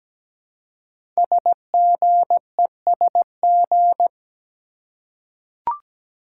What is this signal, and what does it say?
Morse code, heard as a keyed tone, about 17 wpm, 700 Hz. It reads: SGESG
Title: Code Groups - 5 Letters - 17wpm